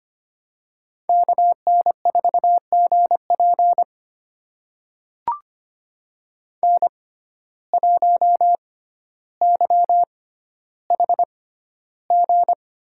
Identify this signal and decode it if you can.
Morse code, heard as a keyed tone, about 25 wpm, spaced as for 13 wpm, 700 Hz. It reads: KN4GP N1YHG
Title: USA Call Signs - 5 Characters - 6x Character Spacing - 25wpm